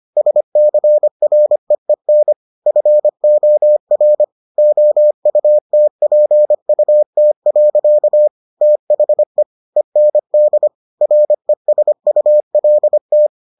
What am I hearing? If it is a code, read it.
Morse code, heard as a keyed tone, about 25 wpm, 600 Hz. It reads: SCREEN FOR OUTPUT. THE END RESULT